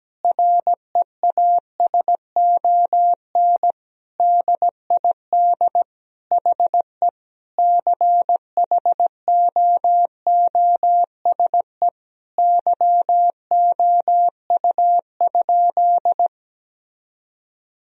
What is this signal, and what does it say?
Morse code, heard as a keyed tone, about 17 wpm, 700 Hz. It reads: REASON DID HE CHOOSE YOU?
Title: Sentences from Top 1000 Words - Review Part 4 - 17wpm